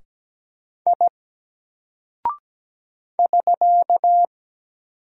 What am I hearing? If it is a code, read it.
Morse code, heard as a keyed tone, about 17 wpm, 700 Hz. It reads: I <SK>
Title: Single Character (Letter:Number:Punctuation:Prosign) 17wpm